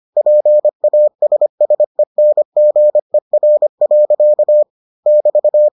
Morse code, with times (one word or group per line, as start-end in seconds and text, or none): 0.2-4.6 PASSENGER.
5.1-5.7 <BT>